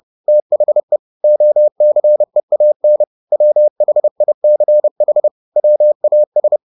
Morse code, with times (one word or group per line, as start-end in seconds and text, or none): 0.3-1.0 THE
1.2-3.0 OCEAN
3.3-5.3 WHICH
5.6-6.6 WAS